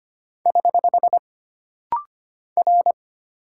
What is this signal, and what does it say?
Morse code, heard as a keyed tone, about 25 wpm, 700 Hz. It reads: <HH> R
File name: Single Character (Letter:Number:Punctuation:Prosign) - Rapid-Fire - 25wpm